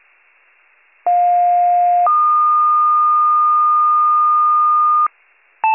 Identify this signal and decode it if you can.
Two-tone page, heard as a one-second tone, then a three-second tone, about 710 then 1220 Hz. It is heard from 1.1 to 5.1 s.